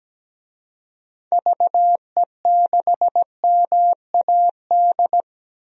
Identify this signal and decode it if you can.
Morse code, heard as a keyed tone, about 17 wpm, 700 Hz. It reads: VE6MAD
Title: DXCC-100 Call Signs - 6 Characters - 17wpm